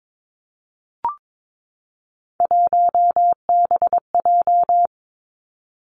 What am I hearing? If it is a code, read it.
Morse code, heard as a keyed tone, about 22 wpm, 700 Hz. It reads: 1BJ
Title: Three Challenging Letter-Number Combinations 22wpm